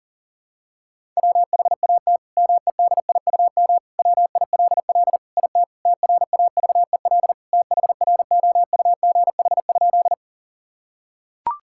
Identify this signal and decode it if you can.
Morse code, heard as a keyed tone, about 40 wpm, 700 Hz. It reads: WHAT MEDIUM WILL IT TRAVEL THROUGH?
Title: Sentences from Top 1000 Words - Review Part 6 - 40wpm